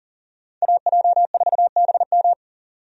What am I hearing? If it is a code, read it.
Morse code, heard as a keyed tone, about 39 wpm, 700 Hz. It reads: AJ4BM
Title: USA Call Signs - 5 Characters - 40wpm